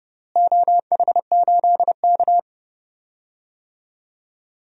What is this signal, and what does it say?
Morse code, heard as a keyed tone, about 30 wpm, 700 Hz. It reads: OH8K
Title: DXCC-100 Call Signs - 4 Characters - Repeated 3x - 30wpm